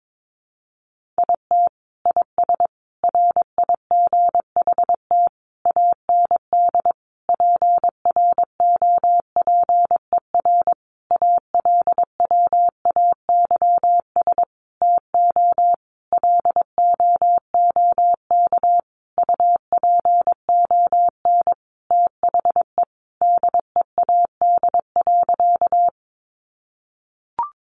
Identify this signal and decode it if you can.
Morse code, heard as a keyed tone, about 22 wpm, 700 Hz. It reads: IT IS RIGHT AND PROPER ALWAYS TO LOOK UPON THE DEAD.
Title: Sentences from Top 1500 Words 22wpm